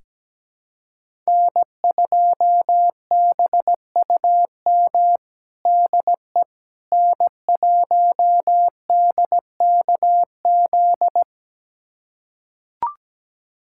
Morse code, heard as a keyed tone, about 17 wpm, 700 Hz. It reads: N2BUM DE N1DKZ
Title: USA Callsign DE Callsign - 5 Characters - 17wpm